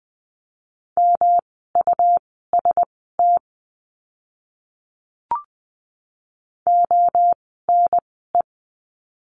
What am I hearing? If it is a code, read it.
Morse code, heard as a keyed tone, about 20 wpm, spaced as for 15 wpm, 700 Hz. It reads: MUST ONE
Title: 237 Lesson 38f - Letter q - Words 20wpm